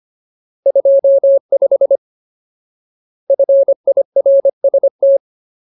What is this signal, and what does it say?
Morse code, heard as a keyed tone, about 25 wpm, 550 Hz. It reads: 25 FIRST
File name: Ultimate Word List / 25WPM Ultimate